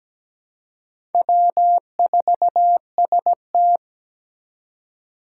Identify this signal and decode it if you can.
Morse code, heard as a keyed tone, about 17 wpm, 700 Hz. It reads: W4ST
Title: Four Challenging Letter-Number Combinations 17wpm